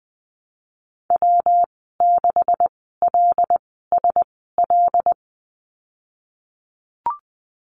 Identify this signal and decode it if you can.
Morse code, heard as a keyed tone, about 20 wpm, spaced as for 15 wpm, 700 Hz. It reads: W6LSL
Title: 245 Lesson 39g - Letter z - Call Signs 20wpm